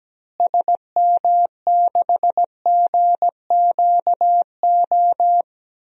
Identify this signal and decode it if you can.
Morse code, heard as a keyed tone, about 17 wpm, 700 Hz. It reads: SM6GQO
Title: DXCC-100 Call Signs - 6 Characters - Repeated 2x - 17wpm